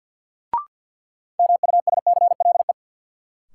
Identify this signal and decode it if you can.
Morse code, heard as a keyed tone, about 53 wpm, 700 Hz. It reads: MUSCLE
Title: Sentences from Top 1300 Words - New Word Drill Alt - 50wpm